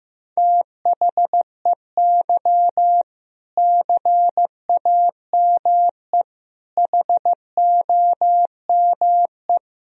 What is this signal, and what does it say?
Morse code, heard as a keyed tone, about 15 wpm, 700 Hz. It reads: THEY CAME HOME
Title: Sentences from Top 1000 Words - Review Part 3 - 15wpm